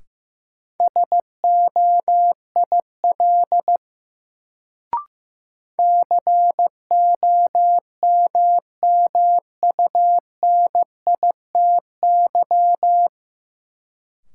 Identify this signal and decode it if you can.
Morse code, heard as a keyed tone, about 15 wpm, 700 Hz. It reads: SOIL COMMUNITY
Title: Sentences from Top 1000 Words - Review Part 4 - New Word Drill - 15wpm